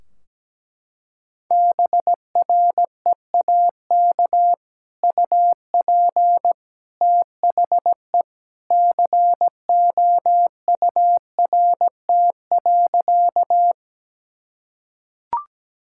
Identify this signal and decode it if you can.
Morse code, heard as a keyed tone, about 17 wpm, 700 Hz. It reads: BREAK UP THE COURT.
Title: Sentences from Top 1600 Words 17wpm